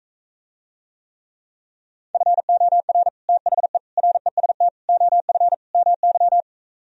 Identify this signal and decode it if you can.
Morse code, heard as a keyed tone, about 45 wpm, 700 Hz. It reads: FOR THE REST OF MY